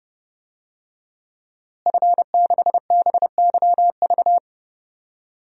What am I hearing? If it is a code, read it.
Morse code, heard as a keyed tone, about 30 wpm, 700 Hz. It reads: F6BYV
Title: DXCC-100 Call Signs - 5 Characters - Repeated 3x - 30wpm